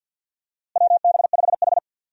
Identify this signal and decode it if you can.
Morse code, heard as a keyed tone, about 53 wpm, 700 Hz. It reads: WB5H
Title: USA Call Signs - 4 Characters - Repeated 3x - 50wpm